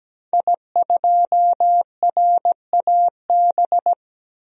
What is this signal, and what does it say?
Morse code, heard as a keyed tone, about 17 wpm, 700 Hz. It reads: I2RAB
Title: DXCC-100 Call Signs - 5 Characters - Repeated 3x - 17wpm